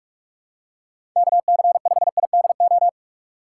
Morse code, heard as a keyed tone, about 46 wpm, 700 Hz. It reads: KC5IDO
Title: USA Call Signs - 6 Characters - 2x Character Spacing - 45wpm